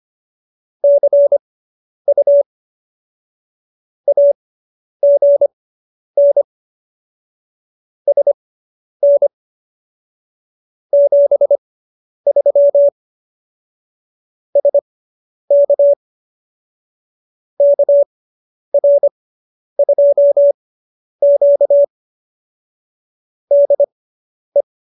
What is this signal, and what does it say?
Morse code, heard as a keyed tone, about 25 wpm, spaced as for 10 wpm, 570 Hz. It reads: CU AGN SN 73 SK KR2Q DE
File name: QSO220 10